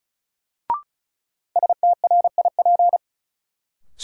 Morse code, heard as a keyed tone, about 36 wpm, 700 Hz. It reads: STRIP
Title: Sentences from Top 1600 Words - New Word Drill - 35wpm